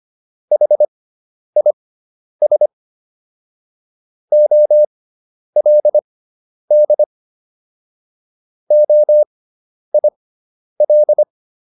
Morse code, heard as a keyed tone, about 25 wpm, spaced as for 10 wpm, 600 Hz. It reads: HIS OLD OIL